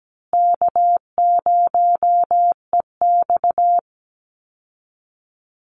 Morse code, heard as a keyed tone, about 17 wpm, 700 Hz. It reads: K0EX
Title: Call Signs Repeated 2x - 17wpm